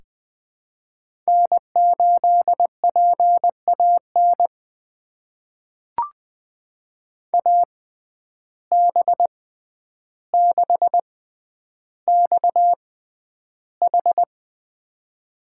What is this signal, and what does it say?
Morse code, heard as a keyed tone, about 20 wpm, spaced as for 10 wpm, 700 Hz. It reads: N8PAN AB6XH
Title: USA Call Signs - 5 Characters - 6x Character Spacing - 20wpm